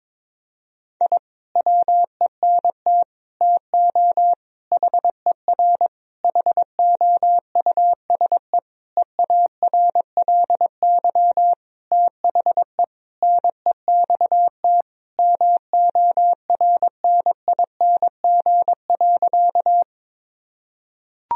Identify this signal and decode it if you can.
Morse code, heard as a keyed tone, about 22 wpm, 700 Hz. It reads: I WENT TO HER HOUSE EARLY THE NEXT MORNING.
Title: Sentences from Top 1000 Words - Review Part 2 - 22wpm